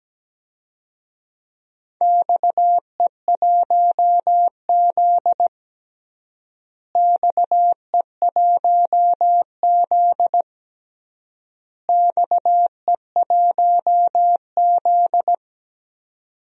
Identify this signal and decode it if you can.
Morse code, heard as a keyed tone, about 17 wpm, 700 Hz. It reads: XE1Z XE1Z XE1Z
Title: DXCC-100 Call Signs - 4 Characters - Repeated 3x - 17wpm